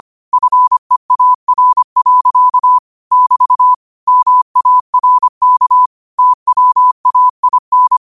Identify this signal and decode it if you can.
Morse code, heard as a keyed tone, about 25 wpm, 1000 Hz. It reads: FEAR. <BT> MARK TWAIN
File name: qotdfiles / 25wpm